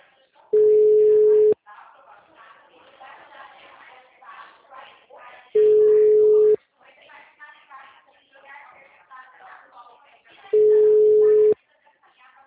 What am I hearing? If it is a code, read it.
Ringback tone (Europe, 425 Hz), with speech.